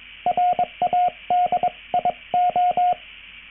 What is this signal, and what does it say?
Morse code, heard as a keyed tone, about 22 wpm, spaced as for 19 wpm, 720 Hz, with radio noise. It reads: RADIO